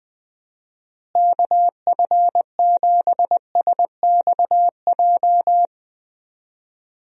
Morse code, heard as a keyed tone, about 21 wpm, 700 Hz. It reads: KF7SXJ
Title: USA Call Signs - 6 Characters - Repeated 2x - 20wpm